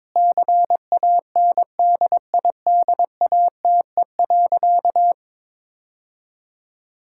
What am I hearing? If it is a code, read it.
Morse code, heard as a keyed tone, about 22 wpm, 700 Hz. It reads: CANDIDATE.
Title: Sentences from Top 1600 Words 22wpm